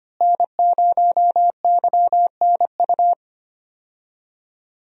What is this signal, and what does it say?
Morse code, heard as a keyed tone, about 25 wpm, 700 Hz. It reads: N0YNU
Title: USA Call Signs - 5 Characters - Repeated 2x - 25wpm